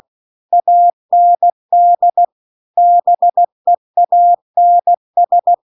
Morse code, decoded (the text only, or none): AND BEANS